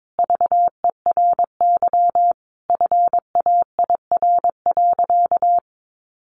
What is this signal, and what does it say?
Morse code, heard as a keyed tone, about 22 wpm, 700 Hz. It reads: VERY FAIR.
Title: Sentences from Top 1300 Words 22wpm